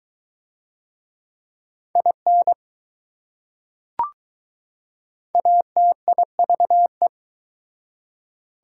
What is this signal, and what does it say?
Morse code, heard as a keyed tone, about 20 wpm, 700 Hz. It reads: IN ATIVE